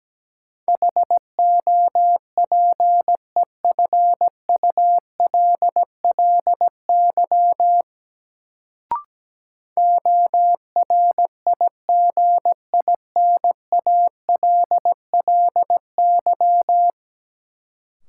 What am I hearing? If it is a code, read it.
Morse code, heard as a keyed tone, about 17 wpm, 700 Hz. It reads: HOPEFULLY ORIGINALLY
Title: Sentences from Top 1500 Words - New Word Drill - 17wpm